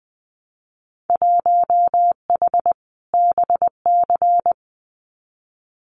Morse code, heard as a keyed tone, about 19 wpm, 700 Hz. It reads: 1H BC
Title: Winter Field Day Exchanges 20wpm